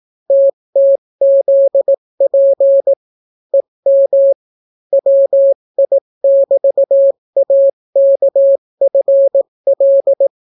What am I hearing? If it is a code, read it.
Morse code, heard as a keyed tone, about 18 wpm, 550 Hz. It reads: TTZP EM WI<BT>AKFL